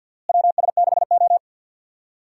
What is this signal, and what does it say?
Morse code, heard as a keyed tone, about 47 wpm, 700 Hz. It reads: WS6O
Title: USA Call Signs - 4 Characters - Repeated 3x - 50wpm